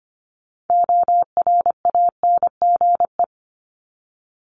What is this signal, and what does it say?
Morse code, heard as a keyed tone, about 25 wpm, 700 Hz. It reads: ORANGE
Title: Top 1751-2000 Words 25wpm